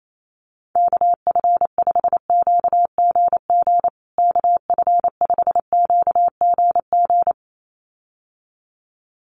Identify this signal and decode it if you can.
Morse code, heard as a keyed tone, about 28 wpm, 700 Hz. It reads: KF5QGG KF5QGG
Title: USA Call Signs Repeated 2x 28wpm